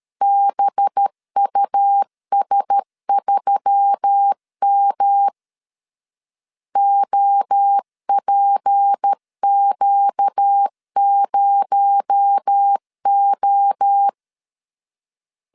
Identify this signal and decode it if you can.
Morse code, heard as a keyed tone, about 13 wpm, 800 Hz. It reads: BUS3M OPQ0O
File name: CW / DET / 1 6 1500C S B 12wpm 3431